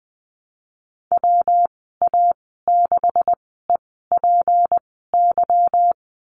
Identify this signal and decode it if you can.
Morse code, heard as a keyed tone, about 20 wpm, spaced as for 15 wpm, 700 Hz. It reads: WA6EPY
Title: 154 Lesson 25g - Letter p - Call Signs 20wpm